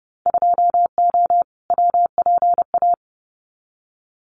Morse code, heard as a keyed tone, about 30 wpm, 700 Hz. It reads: 2O WPA
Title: Winter Field Day Exchanges 30wpm